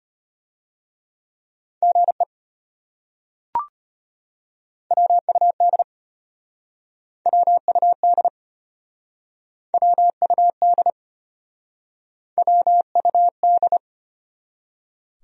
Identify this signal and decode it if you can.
Morse code, heard as a keyed tone, about 33 wpm, 700 Hz. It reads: GE WUD WUD WUD WUD